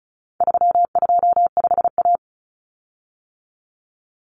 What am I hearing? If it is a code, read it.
Morse code, heard as a keyed tone, about 34 wpm, 700 Hz. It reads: 325A
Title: Common Cut Numbers - 4-Digit 35wpm